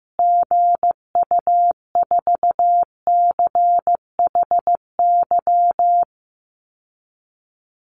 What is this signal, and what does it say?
Morse code, heard as a keyed tone, about 15 wpm, 700 Hz. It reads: GU4CHY
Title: Call Signs Repeated 3x - 15wpm